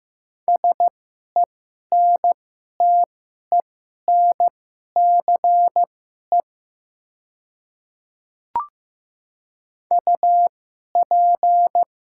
Morse code, heard as a keyed tone, about 15 wpm, spaced as for 11 wpm, 700 Hz. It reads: SENTENCE UP
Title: 2x Character Spacing - Top 200 Words - 15wpm